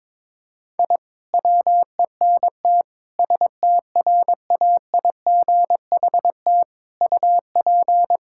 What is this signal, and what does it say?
Morse code, heard as a keyed tone, about 22 wpm, 700 Hz. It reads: I WENT STRAIGHT UP
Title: Sentences from Top 1000 Words - Review Part 6 - 22wpm